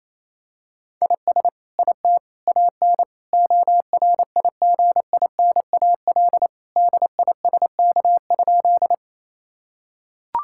Morse code, heard as a keyed tone, about 28 wpm, 700 Hz. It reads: IS IT AN ORIGINAL DISK?
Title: Sentences from Top 1000 Words - Review Part 6 - 28wpm